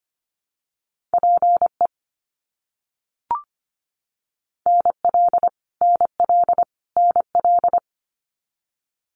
Morse code, heard as a keyed tone, about 25 wpm, 700 Hz. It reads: PE NL NL NL
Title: Canadian Provinces and Territories - Repeated 3x - 25wpm